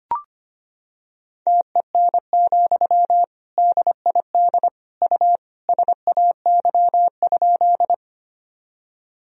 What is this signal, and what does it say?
Morse code, heard as a keyed tone, about 25 wpm, 700 Hz. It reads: TEN, DID U SAY?
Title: Sentences from Top 400 Words and Abbreviations 25wpm